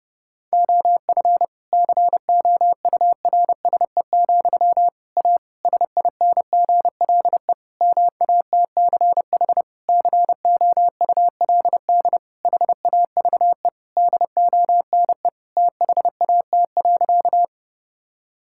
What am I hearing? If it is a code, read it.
Morse code, heard as a keyed tone, about 32 wpm, 700 Hz. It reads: OF COURSE, A SINGLE MATCH COULD HAVE DONE THAT.
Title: Sentences from Top 1600 Words 30wpm